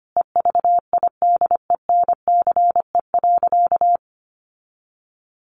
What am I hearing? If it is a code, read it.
Morse code, heard as a keyed tone, about 25 wpm, 700 Hz. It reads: EVIDENCE.
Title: Sentences from Top 1300 Words 25wpm